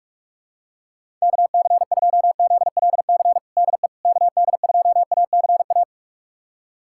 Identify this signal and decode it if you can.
Morse code, heard as a keyed tone, about 46 wpm, 700 Hz. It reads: KC2ZLC DE KD2ACA
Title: USA Callsign DE Callsign 45wpm